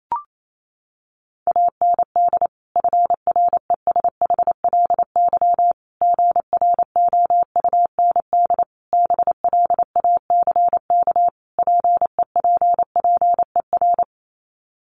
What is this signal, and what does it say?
Morse code, heard as a keyed tone, about 28 wpm, 700 Hz. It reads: AND FRESHLY GROUND BLACK PEPPER
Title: Sets of 5 Words Encore 28wpm